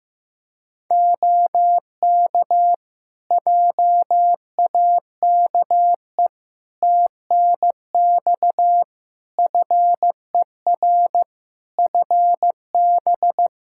Morse code, heard as a keyed tone, about 15 wpm, 700 Hz. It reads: OK JAKE TNX FER FB
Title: Practice QSO - Exchange 4 - Ending - 15wpm